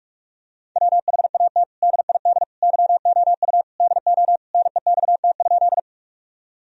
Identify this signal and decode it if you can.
Morse code, heard as a keyed tone, about 46 wpm, 700 Hz. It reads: WHAT DID YOU DO NEXT?
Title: Sentences from Top 1000 Words - Review Part 1 - 45wpm